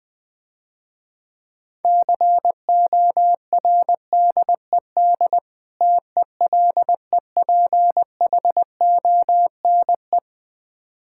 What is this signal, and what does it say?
Morse code, heard as a keyed tone, about 20 wpm, 700 Hz. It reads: CORDED TELEPHONE
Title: Difficult Words - Speed-Racing with Hints - 20wpm